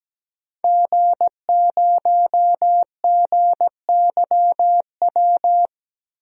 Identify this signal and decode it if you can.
Morse code, heard as a keyed tone, about 17 wpm, 700 Hz. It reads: G0GYW
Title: DXCC-100 Call Signs - 5 Characters - 17wpm